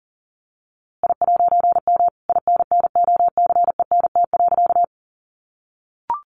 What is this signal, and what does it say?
Morse code, heard as a keyed tone, about 40 wpm, 700 Hz. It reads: I'M INNOCENT.
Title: A Sound Of Thunder 40wpm